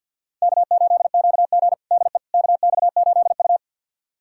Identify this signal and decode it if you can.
Morse code, heard as a keyed tone, about 46 wpm, 700 Hz. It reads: K8QG DE KX8U
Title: USA Callsign DE Callsign - 4 Characters - 50wpm